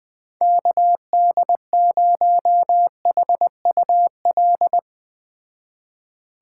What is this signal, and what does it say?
Morse code, heard as a keyed tone, about 20 wpm, 700 Hz. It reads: KD0HUL